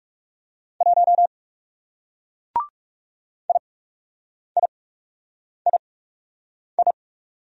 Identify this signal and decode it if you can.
Morse code, heard as a keyed tone, about 40 wpm, spaced as for 8 wpm, 700 Hz. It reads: 1 IIII